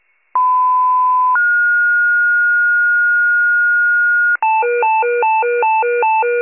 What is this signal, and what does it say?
Two-tone page, heard as a one-second tone, then a three-second tone, about 1010 then 1430 Hz. It is heard from 0.4 to 4.4 s.